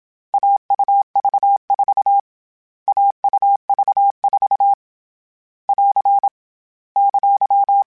Morse code, heard as a keyed tone, about 27 wpm, 800 Hz. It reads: AUV4 AUV4 <AR> !
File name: Short Runs / Short Runs-mp3s / AUV AUV4 @25